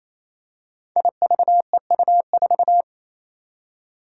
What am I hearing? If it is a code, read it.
Morse code, heard as a keyed tone, about 28 wpm, 700 Hz. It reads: IVEU4
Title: Five Challenging Letter-Number Combinations 28wpm